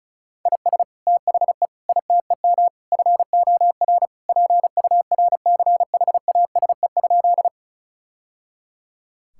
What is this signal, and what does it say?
Morse code, heard as a keyed tone, about 34 wpm, 700 Hz. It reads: IS THE ITEM FOR PURCHASE?
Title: Sentences from Top 1000 Words - Review Part 6 - 35wpm